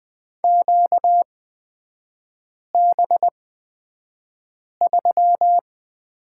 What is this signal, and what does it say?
Morse code, heard as a keyed tone, about 20 wpm, spaced as for 5 wpm, 700 Hz. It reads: QB3
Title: 243 Lesson 39e - Letter z - Three Characters 20wpm